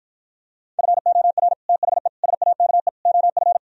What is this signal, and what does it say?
Morse code, heard as a keyed tone, about 52 wpm, 700 Hz. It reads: FOR THE SAKE OF